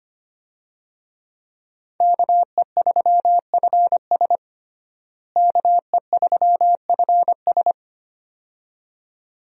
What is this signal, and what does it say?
Morse code, heard as a keyed tone, about 25 wpm, 700 Hz. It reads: KE3FS KE3FS
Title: USA Call Signs - 5 Characters - Repeated 2x - 25wpm